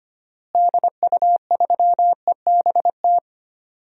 Morse code, heard as a keyed tone, about 25 wpm, 700 Hz. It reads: DU3EBT
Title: DXCC-100 Call Signs - 6 Characters - Repeated 2x - 25wpm